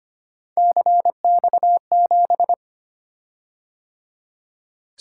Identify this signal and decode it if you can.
Morse code, heard as a keyed tone, about 25 wpm, 700 Hz. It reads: CX7
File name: Three Letter-Number Combinations - 25wpm